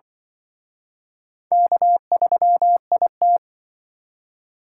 Morse code, heard as a keyed tone, about 25 wpm, 700 Hz. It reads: K3IT
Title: Speed Racing - USA 1x2 Call Signs 20wpm